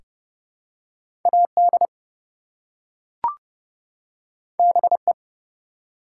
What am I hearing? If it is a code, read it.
Morse code, heard as a keyed tone, about 31 wpm, 700 Hz. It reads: AD BE
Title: Two Letter Words - 30wpm